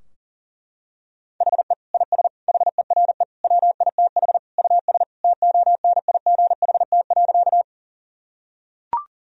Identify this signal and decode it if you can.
Morse code, heard as a keyed tone, about 40 wpm, 700 Hz. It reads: HE IS HERE WITH US TONIGHT.
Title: Sentences from Top 1000 Words - Review Part 3 - 40wpm